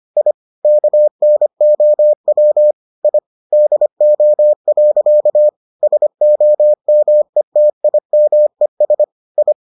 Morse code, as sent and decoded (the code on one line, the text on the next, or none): .. / -.- -. --- .-- / .. / -.. --- .-.-.- / ... --- -- . - .. -- . ... / ..
I KNOW I DO. SOMETIMES I